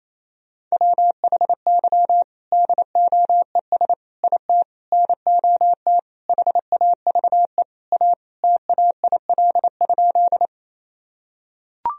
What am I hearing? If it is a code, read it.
Morse code, heard as a keyed tone, about 28 wpm, 700 Hz. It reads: WHY DOES IT NOT HAVE A TAIL?
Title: Sentences from Top 1000 Words - Review Part 3 - 28wpm